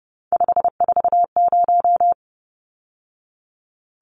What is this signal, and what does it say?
Morse code, heard as a keyed tone, about 30 wpm, 700 Hz. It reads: 540
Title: Numbers - 3-Digit 30wpm